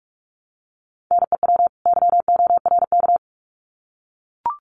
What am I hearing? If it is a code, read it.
Morse code, heard as a keyed tone, about 44 wpm, 700 Hz. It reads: NEW YORK
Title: Sets of 2 Words 45wpm